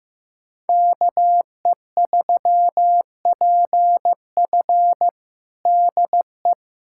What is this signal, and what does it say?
Morse code, heard as a keyed tone, about 15 wpm, 700 Hz. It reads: KE3PF DE
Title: USA Callsign DE Callsign - 5 Characters - 15wpm